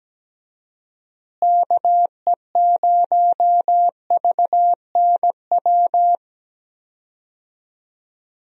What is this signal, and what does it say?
Morse code, heard as a keyed tone, about 17 wpm, 700 Hz. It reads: KE0VNW
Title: USA Call Signs - 6 Characters - Repeated 3x - 17wpm